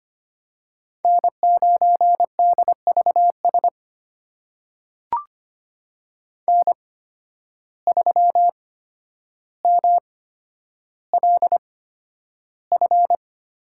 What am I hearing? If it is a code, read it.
Morse code, heard as a keyed tone, about 25 wpm, spaced as for 11 wpm, 700 Hz. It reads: N9DVS N3MLF
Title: USA Call Signs - 5 Characters - 8x Character Spacing - 25wpm